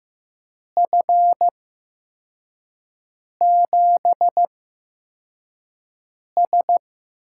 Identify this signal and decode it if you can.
Morse code, heard as a keyed tone, about 15 wpm, spaced as for 4 wpm, 700 Hz. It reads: F7S
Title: USA Call Signs - 4 Characters - 8x Character Spacing - 15wpm